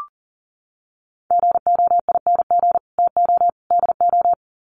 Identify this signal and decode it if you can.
Morse code, heard as a keyed tone, about 40 wpm, 700 Hz. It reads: GOING TO DO
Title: Sets of 3 Words 40wpm